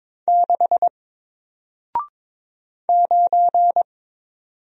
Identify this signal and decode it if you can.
Morse code, heard as a keyed tone, about 22 wpm, 700 Hz. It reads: 6 9
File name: Single-Letter-Number-22wpm